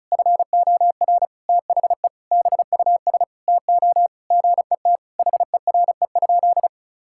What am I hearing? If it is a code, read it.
Morse code, heard as a keyed tone, about 36 wpm, 700 Hz. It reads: FOR THE BUS TO GET HERE?
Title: Sentences from Top 1000 Words - Review Part 6 - 35wpm